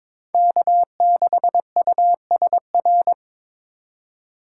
Morse code, heard as a keyed tone, about 26 wpm, 700 Hz. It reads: K6USR
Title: Speed Racing - USA 1x3 Call Signs 22wpm